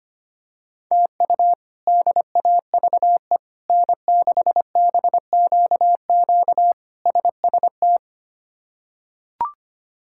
Morse code, heard as a keyed tone, about 25 wpm, 700 Hz. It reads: TU DAVE N6BQQ SST
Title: K1USN SST QSO Practice - 25wpm